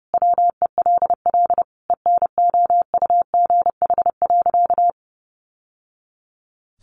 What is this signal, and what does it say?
Morse code, heard as a keyed tone, about 30 wpm, 700 Hz. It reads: WELL ENOUGH.
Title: Sentences from Top 600 Words 30wpm